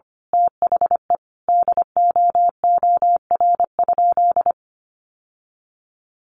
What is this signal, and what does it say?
Morse code, heard as a keyed tone, about 25 wpm, 700 Hz. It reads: THE DOOR?
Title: Sentences from Top 900 Words 25wpm